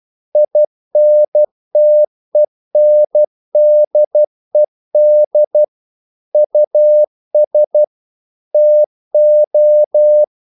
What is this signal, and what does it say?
Morse code, heard as a keyed tone, about 12 wpm, 600 Hz. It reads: INTENDED US TO
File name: CW / Practice / EnglishQuotations0112